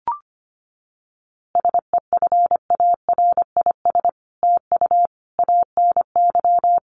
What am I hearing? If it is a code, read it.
Morse code, heard as a keyed tone, about 25 wpm, 700 Hz. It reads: SEFARIS TU ANY
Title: A Sound Of Thunder 25wpm